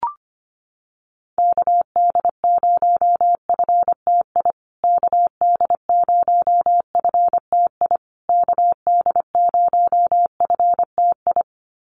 Morse code, heard as a keyed tone, about 25 wpm, 700 Hz. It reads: KD0FTI KD0FTI KD0FTI
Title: USA Call Signs Repeated 3x 25wpm